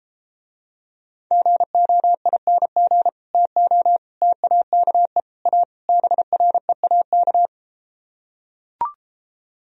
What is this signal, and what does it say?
Morse code, heard as a keyed tone, about 33 wpm, 700 Hz. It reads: GOING TO TAKE A BREAK